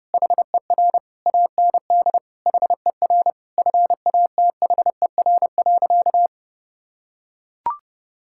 Morse code, heard as a keyed tone, about 30 wpm, 700 Hz. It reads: HER AND HER FATHER.
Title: Sentences from Top 1000 Words - Review Part 6 - 30wpm